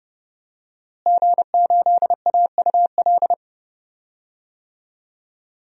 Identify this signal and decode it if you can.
Morse code, heard as a keyed tone, about 32 wpm, 700 Hz. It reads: G8AUL
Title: DXCC-100 Call Signs - 5 Characters - Repeated 2x - 30wpm